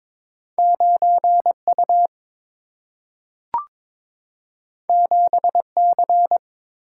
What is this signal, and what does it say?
Morse code, heard as a keyed tone, about 22 wpm, 700 Hz. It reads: 9 U 7 C
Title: Single-Letter Single-Number 22wpm